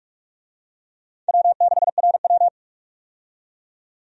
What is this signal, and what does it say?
Morse code, heard as a keyed tone, about 45 wpm, 700 Hz. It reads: W6RW
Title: USA Call Signs - 4 Characters - 45wpm